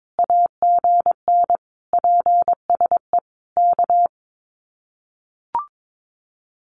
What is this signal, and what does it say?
Morse code, heard as a keyed tone, about 22 wpm, 700 Hz. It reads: AGN PSE K
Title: Common QSO Phrases Encore 22wpm